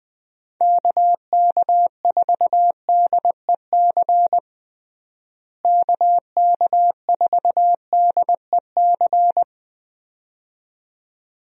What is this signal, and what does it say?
Morse code, heard as a keyed tone, about 20 wpm, 700 Hz. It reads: KK4DEC KK4DEC
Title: USA Call Signs - 6 Characters - Repeated 2x - 20wpm